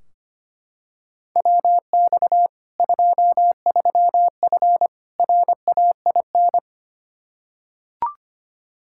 Morse code, heard as a keyed tone, about 25 wpm, 700 Hz. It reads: WX 23F RAIN
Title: Practice QSO - Exchange 2 - Info - 25wpm